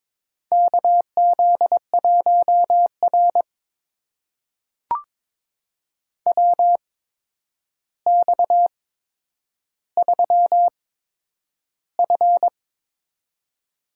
Morse code, heard as a keyed tone, about 22 wpm, spaced as for 9 wpm, 700 Hz. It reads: KZ1R WX3F
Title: USA Call Signs - 4 Characters - 8x Character Spacing - 22wpm